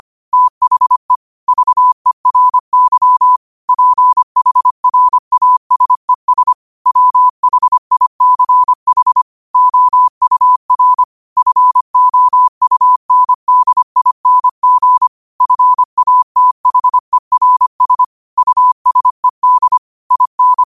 Morse code, heard as a keyed tone, about 25 wpm, 1000 Hz. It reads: THE VERY PHRASES WHICH OUR FOUNDING FATHERS USED IN